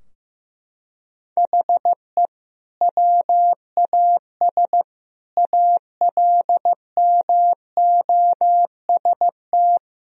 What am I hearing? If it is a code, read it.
Morse code, heard as a keyed tone, about 15 wpm, 700 Hz. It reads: HE WAS ALMOST
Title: Sentences from Top 1000 Words - Review Part 4 - 15wpm